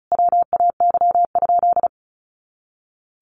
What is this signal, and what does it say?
Morse code, heard as a keyed tone, about 34 wpm, 700 Hz. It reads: WAY?
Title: Sentences from Top 200 Words 35wpm